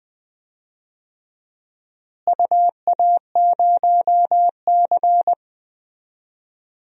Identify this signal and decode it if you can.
Morse code, heard as a keyed tone, about 20 wpm, 700 Hz. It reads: UA0C
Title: DXCC-100 Call Signs - 4 Characters - Repeated 2x - 20wpm